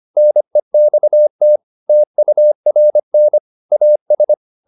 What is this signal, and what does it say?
Morse code, heard as a keyed tone, about 25 wpm, 600 Hz. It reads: NEXT TURN AS